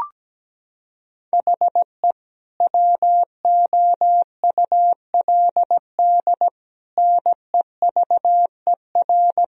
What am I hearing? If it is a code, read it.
Morse code, heard as a keyed tone, about 17 wpm, 700 Hz. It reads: HE WOULD NEVER
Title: Sentences from Top 1000 Words - Review Part 1 - 17wpm